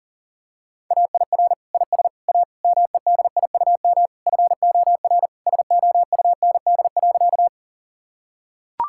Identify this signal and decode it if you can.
Morse code, heard as a keyed tone, about 40 wpm, 700 Hz. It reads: AIR IS A MEDIUM FOR SOUND.
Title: Sentences from Top 1000 Words - Review Part 6 - 40wpm